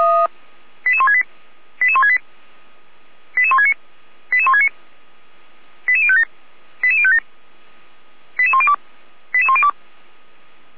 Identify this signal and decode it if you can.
Five-tone selective call (ZVEI), heard as quick runs of five tones, tones of about 70 ms. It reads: 88168 88168 88169 88169 88057 88057 88102 88102